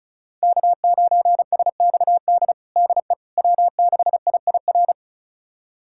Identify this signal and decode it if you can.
Morse code, heard as a keyed tone, about 34 wpm, 700 Hz. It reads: K9SXD DE W6IIR
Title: USA Callsign DE Callsign - 5 Characters - 35wpm